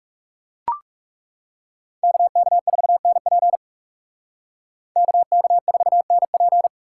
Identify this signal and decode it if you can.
Morse code, heard as a keyed tone, about 38 wpm, 700 Hz. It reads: KK4NP KK4NP